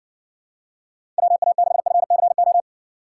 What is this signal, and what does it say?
Morse code, heard as a keyed tone, about 60 wpm, 700 Hz. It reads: WA6FCO